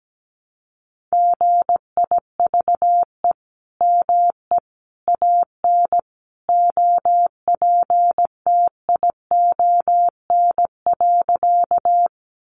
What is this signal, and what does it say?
Morse code, heard as a keyed tone, about 17 wpm, 700 Hz. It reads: GIVE ME AN OPTION.
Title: Sentences from Top 1300 Words 17wpm